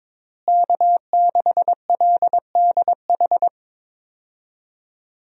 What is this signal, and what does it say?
Morse code, heard as a keyed tone, about 22 wpm, 700 Hz. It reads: K6LDH
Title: USA Call Signs - 5 Characters - Repeated 2x - 22wpm